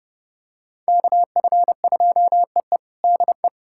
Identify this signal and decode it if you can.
Morse code, heard as a keyed tone, about 30 wpm, 700 Hz. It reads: KF2EE DE
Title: USA Callsign DE Callsign - 5 Characters - 30wpm